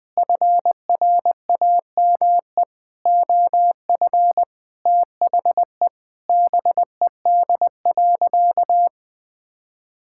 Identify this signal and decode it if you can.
Morse code, heard as a keyed tone, about 20 wpm, 700 Hz. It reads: FRAME OF THE BED.
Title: Sentences from Top 1000 Words - Review Part 6 - 20wpm